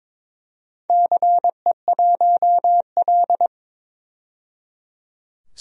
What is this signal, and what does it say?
Morse code, heard as a keyed tone, about 22 wpm, 700 Hz. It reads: CE1L
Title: DXCC-100 Call Signs - 4 Characters - 22wpm